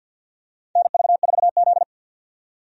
Morse code, heard as a keyed tone, about 53 wpm, 700 Hz. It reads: NV4Z
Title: USA Call Signs - 4 Characters - 50wpm